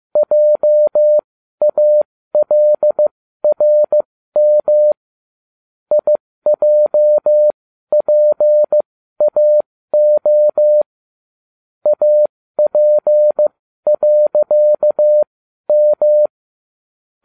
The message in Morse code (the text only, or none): JALRM IJPAO AP.M